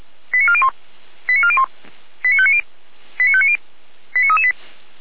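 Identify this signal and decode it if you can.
Five-tone selective call (ZVEI), heard as quick runs of five tones, tones of about 70 ms. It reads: 89401 89401 89590 89590 89338
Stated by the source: ZVEI five-tone selective call received by SDR.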